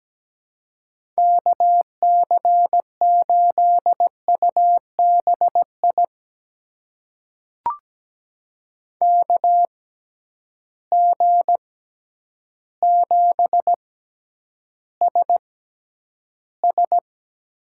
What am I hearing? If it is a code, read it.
Morse code, heard as a keyed tone, about 17 wpm, spaced as for 9 wpm, 700 Hz. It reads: KC8UBI KG7SS
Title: USA Call Signs - 6 Characters - 6x Character Spacing - 17wpm